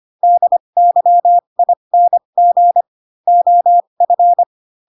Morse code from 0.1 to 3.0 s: DYING